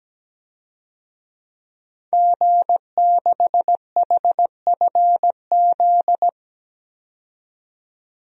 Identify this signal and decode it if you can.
Morse code, heard as a keyed tone, about 17 wpm, 700 Hz. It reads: G6HFZ
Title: DXCC-100 Call Signs - 5 Characters - 17wpm